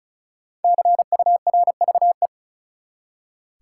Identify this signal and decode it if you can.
Morse code, heard as a keyed tone, about 36 wpm, 700 Hz. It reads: CURVE
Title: Five Letter Words - 35wpm